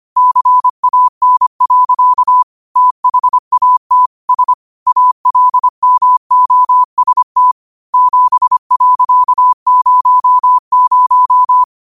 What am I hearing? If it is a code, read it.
Morse code, heard as a keyed tone, about 25 wpm, 1000 Hz. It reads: CAN. THAT S ALMOST 7.00
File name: qotdfiles / 25wpm